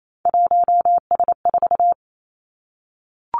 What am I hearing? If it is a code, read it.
Morse code, heard as a keyed tone, about 28 wpm, 700 Hz. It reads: 1S4
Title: Three Challenging Letter-Number Combinations 28wpm